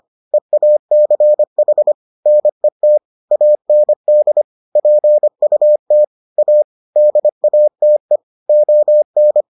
Morse code, heard as a keyed tone, about 25 wpm, 600 Hz. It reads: EACH NET AND PUT A DATE ON